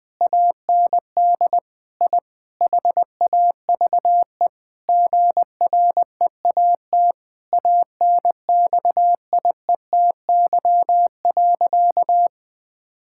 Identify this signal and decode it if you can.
Morse code, heard as a keyed tone, about 20 wpm, 700 Hz. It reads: AND I HAVE GREAT ANXIETY.
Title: Sentences from Top 1600 Words 20wpm